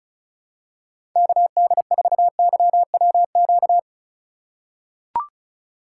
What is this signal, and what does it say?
Morse code, heard as a keyed tone, about 36 wpm, 700 Hz. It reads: KD4YWQ
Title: USA Call Signs - 6 Characters - 4x Character Spacing - 35wpm